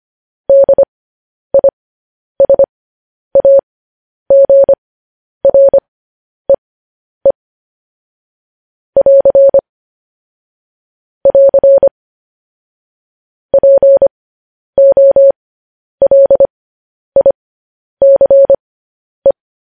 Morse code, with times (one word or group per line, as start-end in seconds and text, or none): 0.5-7.3 DISAGREE
9.0-9.6 <AR>
11.3-11.9 <AR>
13.5-19.3 POLICE